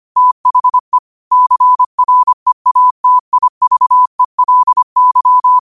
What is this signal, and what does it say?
Morse code, heard as a keyed tone, about 25 wpm, 1000 Hz. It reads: THE CREATIVELY
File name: qotdfiles / 25wpm